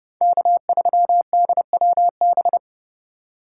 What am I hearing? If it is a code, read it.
Morse code, heard as a keyed tone, about 36 wpm, 700 Hz. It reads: K3DWB
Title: Speed Racing - USA 1x3 Call Signs 30wpm